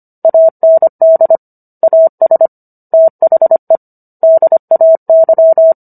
Morse code, heard as a keyed tone, about 25 wpm, 650 Hz. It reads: AND AS THE DAY